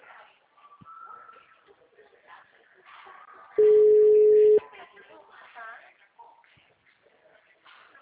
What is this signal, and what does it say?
Ringback tone (Europe, 425 Hz), with speech.
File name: out- -2010- - - .41722